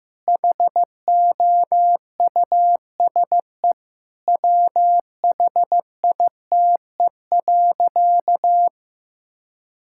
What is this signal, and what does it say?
Morse code, heard as a keyed tone, about 14 wpm, 700 Hz. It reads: HOUSE WHITE.
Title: Sentences from Top 1000 Words - Review Part 6 - 15wpm